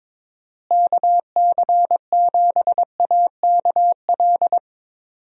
Morse code, heard as a keyed tone, about 22 wpm, 700 Hz. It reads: KC7AKL
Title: USA Call Signs - 6 Characters - Repeated 2x - 22wpm